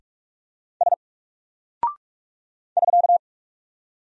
Morse code, heard as a keyed tone, about 43 wpm, 700 Hz. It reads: S <SK>
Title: Single Character (Letter:Number:Punctuation:Prosign) Alt 45wpm